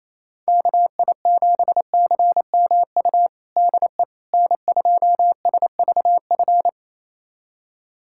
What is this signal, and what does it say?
Morse code, heard as a keyed tone, about 28 wpm, 700 Hz. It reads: KI7CMU DE N2SVF
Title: USA Callsign DE Callsign 28wpm